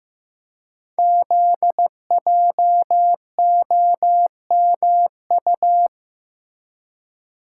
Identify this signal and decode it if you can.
Morse code, heard as a keyed tone, about 15 wpm, 700 Hz. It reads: ZJOMU
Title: Code Groups - 5 Letters - 15wpm